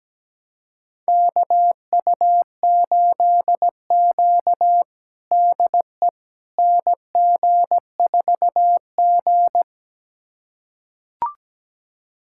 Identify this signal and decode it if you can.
Morse code, heard as a keyed tone, about 17 wpm, 700 Hz. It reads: KU8Q DE NG4G
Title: USA Callsign DE Callsign - 4 Characters - 17wpm